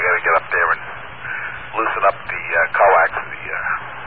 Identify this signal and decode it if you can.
Emergency Alert System (SAME) tones, heard as harsh buzzing bursts, with speech. It is heard from 1.1 to 3.8 s.